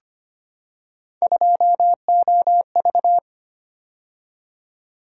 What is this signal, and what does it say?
Morse code, heard as a keyed tone, about 25 wpm, 700 Hz. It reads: 2OV
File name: Three Letter-Number Combinations - 25wpm